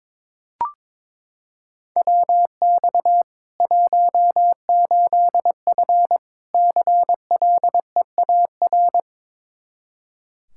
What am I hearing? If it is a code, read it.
Morse code, heard as a keyed tone, about 22 wpm, 700 Hz. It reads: WX 18F CLEAR
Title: Practice QSO - Exchange 2 - Info - 22wpm